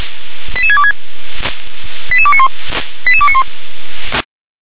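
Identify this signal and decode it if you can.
Five-tone selective call (ZVEI), heard as quick runs of five tones, tones of about 70 ms. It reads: 88626 88291 88291
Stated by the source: ZVEI five-tone selective call 88741 received on an SDR.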